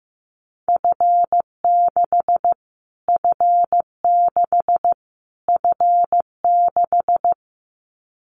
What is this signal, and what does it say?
Morse code, heard as a keyed tone, about 15 wpm, 700 Hz. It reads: F6 F6 F6
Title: Callsign Prefixes for Top 25 DXCC - Repeated 3x - 15wpm